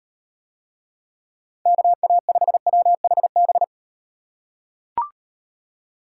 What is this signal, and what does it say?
Morse code, heard as a keyed tone, about 35 wpm, 700 Hz. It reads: KA5WHB